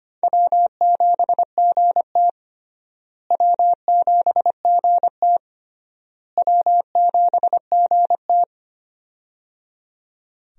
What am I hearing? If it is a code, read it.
Morse code, heard as a keyed tone, about 25 wpm, 700 Hz. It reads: W7GT W7GT W7GT
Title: USA Call Signs - 4 Characters - Repeated 3x - 25wpm